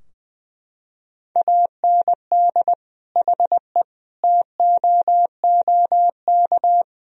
Morse code, heard as a keyed tone, about 20 wpm, 700 Hz. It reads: AND HE TOOK
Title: Sentences from Top 1000 Words - Review Part 4 - 20wpm